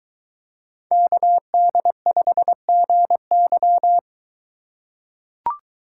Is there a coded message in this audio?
Yes: Morse code from 0.9 to 4.0 s, heard as a keyed tone, about 22 wpm, 700 Hz. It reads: KD5GY